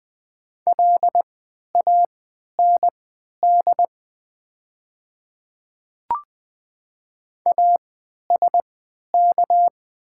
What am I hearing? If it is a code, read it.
Morse code, heard as a keyed tone, about 20 wpm, spaced as for 11 wpm, 700 Hz. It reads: LAND ASK
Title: 3x Character Spacing - Top 200 Words - 20wpm